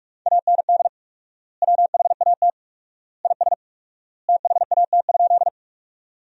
Morse code, heard as a keyed tone, about 46 wpm, 700 Hz. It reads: AND WHAT IS THAT?
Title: 4x Word-Spacing - Sentences from Top 100 Words - 45wpm